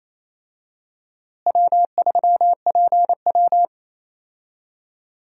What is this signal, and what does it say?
Morse code, heard as a keyed tone, about 28 wpm, 700 Hz. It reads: W3PW
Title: USA Call Signs - 4 Characters - Repeated 3x - 28wpm